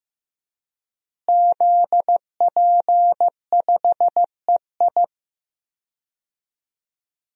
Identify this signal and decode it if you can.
Morse code, heard as a keyed tone, about 15 wpm, 700 Hz. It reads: ZP5EI
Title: DXCC-100 Call Signs - 5 Characters - Repeated 2x - 15wpm